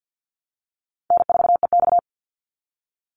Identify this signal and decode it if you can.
Morse code, heard as a keyed tone, about 48 wpm, 700 Hz. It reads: N4EX
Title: NA SOTA Activator Callsigns 50wpm